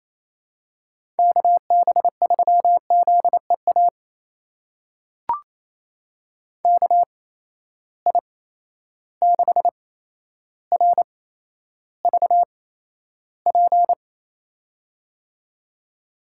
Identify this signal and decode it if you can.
Morse code, heard as a keyed tone, about 28 wpm, spaced as for 12 wpm, 700 Hz. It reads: KB3ZEA KI6RVP
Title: USA Call Signs - 6 Characters - 8x Character Spacing - 28wpm